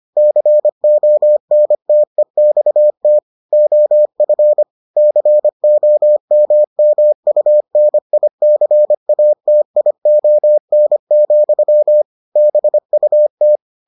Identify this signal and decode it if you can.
Morse code, heard as a keyed tone, about 25 wpm, 600 Hz. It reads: CONTEXT OF COMMUNICATION, BUT